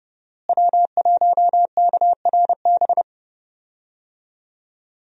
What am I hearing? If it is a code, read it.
Morse code, heard as a keyed tone, about 30 wpm, 700 Hz. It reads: W1KRB
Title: USA Call Signs - 5 Characters - Repeated 2x - 30wpm